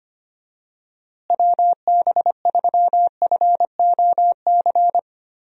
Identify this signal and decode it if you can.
Morse code, heard as a keyed tone, about 25 wpm, 700 Hz. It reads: WB3FOC
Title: USA Call Signs - 6 Characters - Repeated 2x - 25wpm